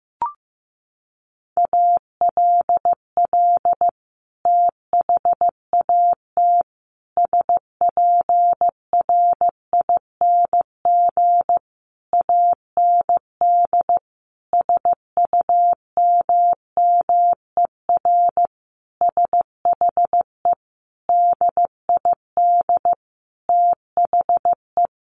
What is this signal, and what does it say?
Morse code, heard as a keyed tone, about 15 wpm, 700 Hz. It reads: ALL THAT SPRING AND SUMMER SHE DID THE
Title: Sentences from Top 1200 Words 15wpm